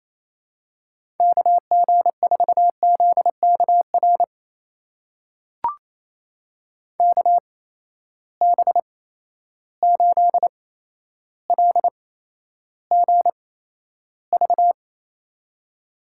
Morse code, heard as a keyed tone, about 28 wpm, spaced as for 12 wpm, 700 Hz. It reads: KG4ZKR KB8LGV
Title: USA Call Signs - 6 Characters - 8x Character Spacing - 28wpm